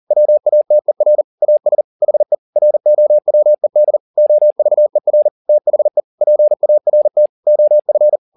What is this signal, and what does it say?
Morse code, heard as a keyed tone, about 40 wpm, 600 Hz. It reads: WATER AS HE ROWED OVER THE PART OF